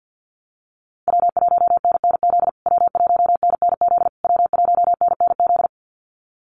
Morse code, heard as a keyed tone, about 50 wpm, 700 Hz. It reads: W1NNZ W1NNZ W1NNZ
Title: USA Call Signs Repeated 3x 50wpm